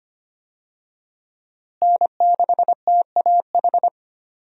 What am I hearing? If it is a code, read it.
Morse code, heard as a keyed tone, about 25 wpm, 700 Hz. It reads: N6TAH
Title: USA Call Signs - 5 Characters - Repeated 2x - 25wpm